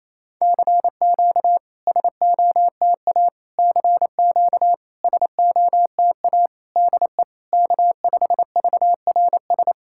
Morse code, heard as a keyed tone, about 28 wpm, 700 Hz. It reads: CQ SOTA CQ SOTA DE K5VRS
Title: SOTA QSO Practice 28wpm